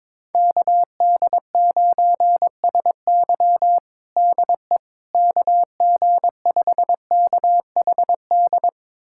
Morse code, heard as a keyed tone, about 22 wpm, 700 Hz. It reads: KD9SY DE KG5KHD
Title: USA Callsign DE Callsign 22wpm